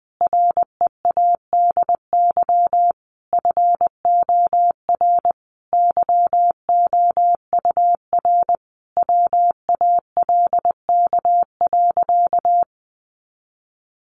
Morse code, heard as a keyed tone, about 20 wpm, 700 Hz. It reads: READY FOR YOUR WALK.
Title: Sentences from Top 800 Words 20wpm